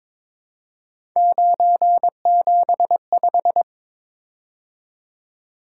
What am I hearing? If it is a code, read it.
Morse code, heard as a keyed tone, about 23 wpm, 700 Hz. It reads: 975
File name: practice / 22wpm / Three Letter-Number Combinations - 22wpm